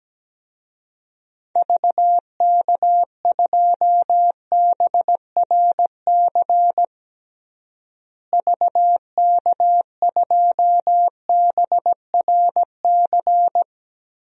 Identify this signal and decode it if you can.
Morse code, heard as a keyed tone, about 17 wpm, 700 Hz. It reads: VK2BRC VK2BRC
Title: DXCC-100 Call Signs - 6 Characters - Repeated 3x - 17wpm